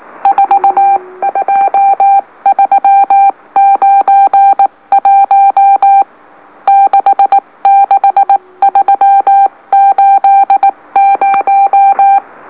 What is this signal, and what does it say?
Morse code, heard as a keyed tone, about 18 wpm, 790 Hz. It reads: 42391 66380
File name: M04- -2345utc-LOLO